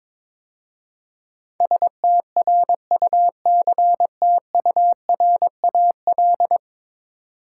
Morse code, heard as a keyed tone, about 22 wpm, 700 Hz. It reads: STRUCTURAL
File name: practice / 22wpm / 10 Letter Words - First 3 Letters Given - 22wpm